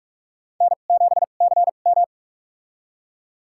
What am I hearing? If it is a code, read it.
Morse code, heard as a keyed tone, about 46 wpm, spaced as for 33 wpm, 700 Hz. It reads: N7CM
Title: USA Call Signs - 4 Characters - 2x Character Spacing - 45wpm